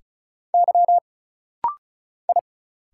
Morse code, heard as a keyed tone, about 34 wpm, 700 Hz. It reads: Y I
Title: Single Letters - Rapid-Fire 35wpm